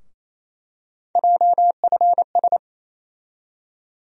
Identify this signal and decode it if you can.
Morse code, heard as a keyed tone, about 28 wpm, 700 Hz. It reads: JFS
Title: USA Call Sign Postfixes - 3 Letters 28wpm